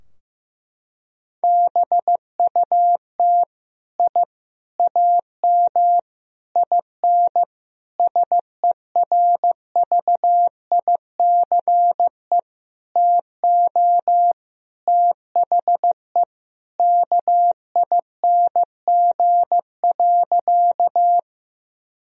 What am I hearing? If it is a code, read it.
Morse code, heard as a keyed tone, about 15 wpm, 700 Hz. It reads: BUT I AM IN SERVICE TO THE KING.
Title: Sentences from Top 1000 Words - Review Part 4 - 15wpm